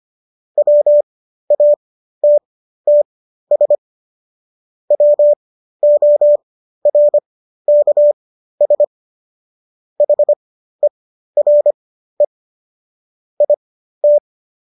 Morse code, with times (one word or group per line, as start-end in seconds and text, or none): 0.6-3.8 WATTS
4.9-8.8 WORKS
10.0-12.2 HERE
13.4-14.2 IT